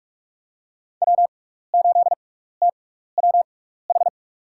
Morse code, heard as a keyed tone, about 46 wpm, spaced as for 15 wpm, 690 Hz. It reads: W8TWH
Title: USA Call Signs - 5 Characters - 6x Character Spacing - 45wpm